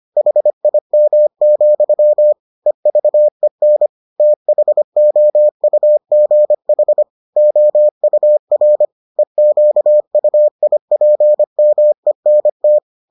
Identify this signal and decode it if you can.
Morse code, heard as a keyed tone, about 25 wpm, 600 Hz. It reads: HIM, EVEN THOUGH OUR EQUIPMENT